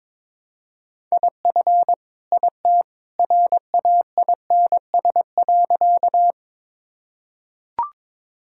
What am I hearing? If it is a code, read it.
Morse code, heard as a keyed tone, about 22 wpm, 700 Hz. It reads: IF IT RAINS.
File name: practice / 22wpm / Sentences with 2-Word Phrases - 22wpm